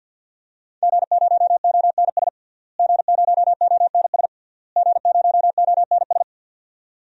Morse code, heard as a keyed tone, about 52 wpm, 700 Hz. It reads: G0ONS G0ONS G0ONS
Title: DXCC-100 Call Signs - 5 Characters - Repeated 3x - 50wpm